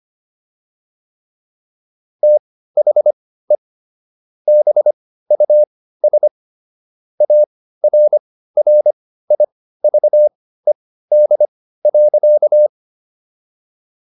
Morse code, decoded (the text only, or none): THE BUS ARRIVED.